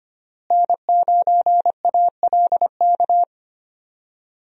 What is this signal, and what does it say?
Morse code, heard as a keyed tone, about 25 wpm, 700 Hz. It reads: N9ALK
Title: USA Call Signs - 5 Characters - Repeated 3x - 25wpm